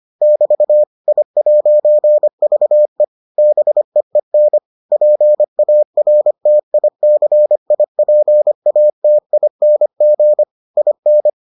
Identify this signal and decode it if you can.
Morse code, heard as a keyed tone, about 25 wpm, 600 Hz. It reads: <BT> I'VE BEEN PARTICIPATING IN